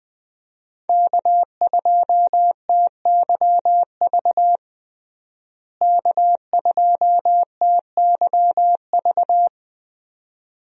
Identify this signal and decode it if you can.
Morse code, heard as a keyed tone, about 20 wpm, 700 Hz. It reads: K2TYV K2TYV
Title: USA Call Signs - 5 Characters - Repeated 2x - 20wpm